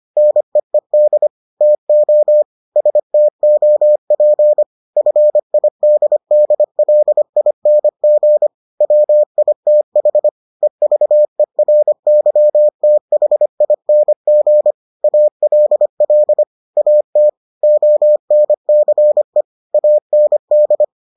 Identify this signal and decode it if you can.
Morse code, heard as a keyed tone, about 25 wpm, 600 Hz. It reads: NEED TO STOP FIDDLING WITH EVERYTHING ALL AT ONCE AND